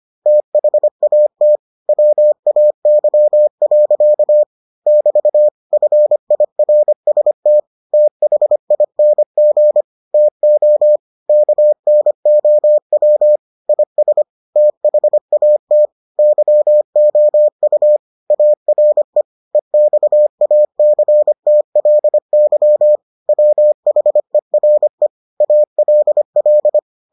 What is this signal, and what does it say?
Morse code, heard as a keyed tone, about 25 wpm, 600 Hz. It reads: THAT WAY. <BT> FIRST THING TO KNOW IS THAT YOU ARE EXACTLY WHERE ALL